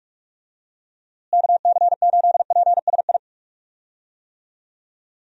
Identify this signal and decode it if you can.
Morse code, heard as a keyed tone, about 46 wpm, 700 Hz. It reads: KC8PSI
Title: USA Call Signs - 6 Characters - Repeated 3x - 45wpm